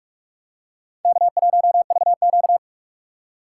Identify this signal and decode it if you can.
Morse code, heard as a keyed tone, about 41 wpm, 700 Hz. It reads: K1VQ